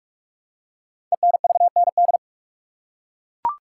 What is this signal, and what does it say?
Morse code, heard as a keyed tone, about 44 wpm, 700 Hz. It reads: ENVND
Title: Code Groups - 5 Letters - 45wpm